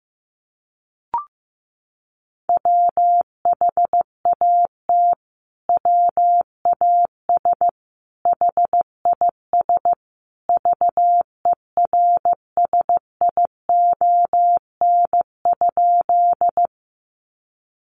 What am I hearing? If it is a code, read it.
Morse code, heard as a keyed tone, about 15 wpm, 700 Hz. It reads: WHAT WAS HIS VERSION?
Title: Sentences from Top 1400 Words 15wpm